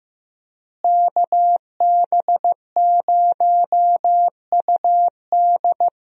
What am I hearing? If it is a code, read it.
Morse code, heard as a keyed tone, about 15 wpm, 700 Hz. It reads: KB0UD
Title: USA Call Signs - 5 Characters - Repeated 3x - 15wpm